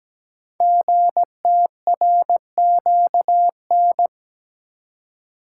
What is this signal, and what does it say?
Morse code, heard as a keyed tone, about 17 wpm, 700 Hz. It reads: GTRQN
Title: Code Groups - 5 Letters - 17wpm